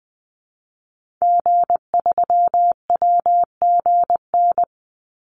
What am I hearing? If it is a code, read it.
Morse code, heard as a keyed tone, about 20 wpm, 700 Hz. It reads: G3WGN
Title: Call Signs - 20wpm